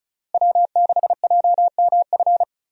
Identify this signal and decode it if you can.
Morse code, heard as a keyed tone, about 36 wpm, 700 Hz. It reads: W6JMF
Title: USA Call Signs - 5 Characters - Repeated 3x - 35wpm